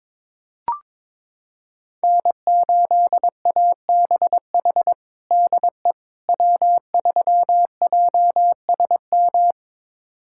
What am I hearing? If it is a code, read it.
Morse code, heard as a keyed tone, about 21 wpm, 700 Hz. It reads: N8ABH DE W3JSM
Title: USA Callsign DE Callsign - 5 Characters - 22wpm